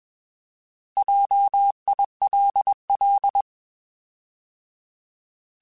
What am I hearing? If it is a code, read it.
Morse code, heard as a keyed tone, about 21 wpm, 800 Hz. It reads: JILL
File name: Alphabetical / Alphabetical mp3s / Js @20